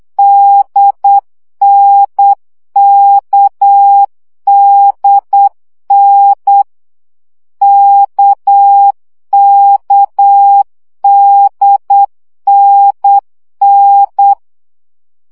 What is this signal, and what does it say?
Morse code, heard as a keyed tone, about 8 wpm, 800 Hz. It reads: DNKDN KKDNN
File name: CW gyakorlas n d k 7 baud